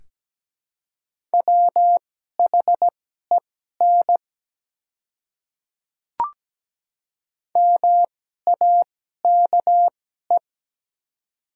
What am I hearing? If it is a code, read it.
Morse code, heard as a keyed tone, about 17 wpm, spaced as for 12 wpm, 700 Hz. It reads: WHEN MAKE
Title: 2x Character Spacing - Top 100 Words - 17wpm